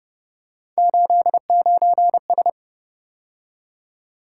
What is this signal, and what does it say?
Morse code, heard as a keyed tone, about 31 wpm, 700 Hz. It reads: 89S